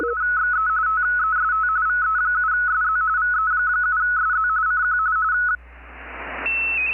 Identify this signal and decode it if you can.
RTTY, about 45.45 baud. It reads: XE TEST VE7SZ VE7SZ CQ
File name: 20m-rtty-1